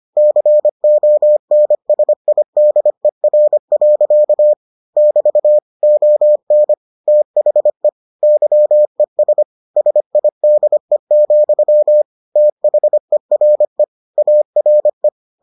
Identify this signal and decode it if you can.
Morse code, heard as a keyed tone, about 25 wpm, 600 Hz. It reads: CONSIDER. <BT> ON THE YES SIDE, THERE ARE